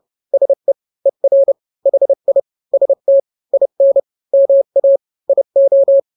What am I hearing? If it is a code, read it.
Morse code, heard as a keyed tone, about 30 wpm, 550 Hz. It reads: SE ER HI ST IN MA IO
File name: n-grams / 2-grams 30 25 WPM